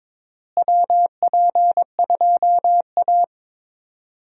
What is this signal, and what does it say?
Morse code, heard as a keyed tone, about 22 wpm, 700 Hz. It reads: WP2A
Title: USA Call Signs - 4 Characters - Repeated 3x - 22wpm